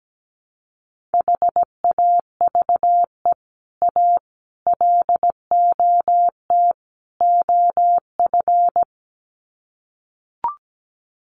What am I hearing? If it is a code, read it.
Morse code, heard as a keyed tone, about 17 wpm, 700 Hz. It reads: HAVE A LOT OF
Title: Sets of 4 Words 17wpm